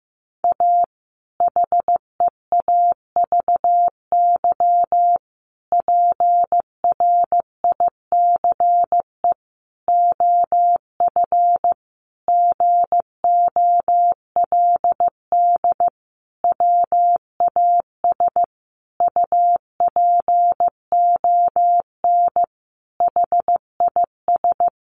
Morse code, heard as a keyed tone, about 15 wpm, 700 Hz. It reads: A HEAVY PRICE OF GOLD WAS UPON HIS
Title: Sentences from Top 1100 Words 15wpm